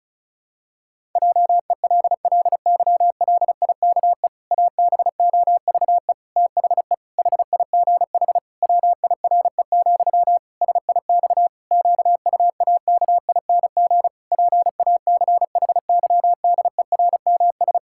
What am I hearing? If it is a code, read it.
Morse code, heard as a keyed tone, about 36 wpm, 700 Hz. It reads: JELLYLIKE ABOVE THE HIGH WIRE, SIX QUAKING PACHYDERMS
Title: Pangram Challenge v2 - 35wpm